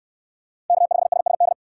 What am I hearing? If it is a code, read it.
Morse code, heard as a keyed tone, about 64 wpm, 700 Hz. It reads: K5SSD